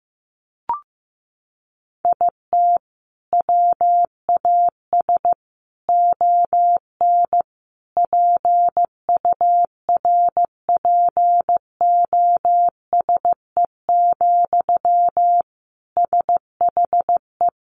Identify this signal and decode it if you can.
Morse code, heard as a keyed tone, about 15 wpm, 700 Hz. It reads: IT WAS ON PURPOSE, SHE
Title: Sentences from Top 800 Words 15wpm